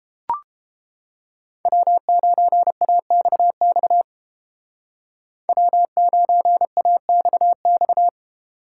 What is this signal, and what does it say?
Morse code, heard as a keyed tone, about 32 wpm, 700 Hz. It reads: W9AXX W9AXX